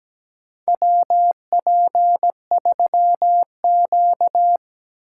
Morse code, heard as a keyed tone, about 17 wpm, 700 Hz. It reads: WP3Q
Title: USA Call Signs - 4 Characters - 2x Character Spacing - 17wpm